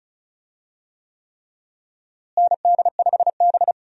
Morse code, heard as a keyed tone, about 36 wpm, 700 Hz. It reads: ND5B
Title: USA Call Signs - 4 Characters - Repeated 3x - 35wpm